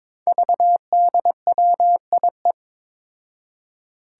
Morse code, heard as a keyed tone, about 22 wpm, 700 Hz. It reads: VDWIE
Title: Five Challenging Letter-Number Combinations 22wpm